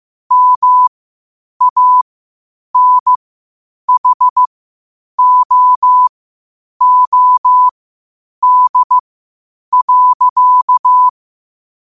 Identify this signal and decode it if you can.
Morse code, heard as a keyed tone, about 16 wpm, spaced as for 8 wpm, 1000 Hz. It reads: MANHOOD.
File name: qotdfiles / 8wpm